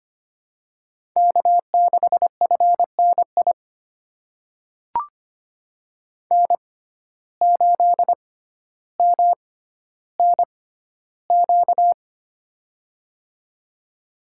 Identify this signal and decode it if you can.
Morse code, heard as a keyed tone, about 25 wpm, spaced as for 13 wpm, 700 Hz. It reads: K6FNI N8MNQ
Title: USA Call Signs - 5 Characters - 6x Character Spacing - 25wpm